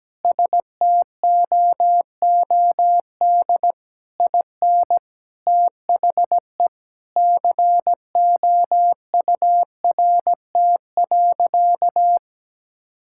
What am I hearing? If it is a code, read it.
Morse code, heard as a keyed tone, about 17 wpm, 700 Hz. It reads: STOOD IN THE COURT.
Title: Sentences from Top 1600 Words 17wpm